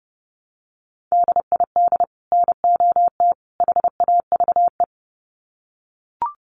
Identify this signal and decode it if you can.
Morse code, heard as a keyed tone, about 30 wpm, 700 Hz. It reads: DID NOT HAVE
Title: Sets of 3 Words 30wpm